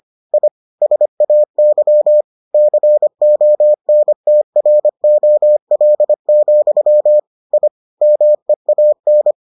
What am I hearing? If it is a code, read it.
Morse code, heard as a keyed tone, about 25 wpm, 600 Hz. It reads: I SAY CONTROL, I MEAN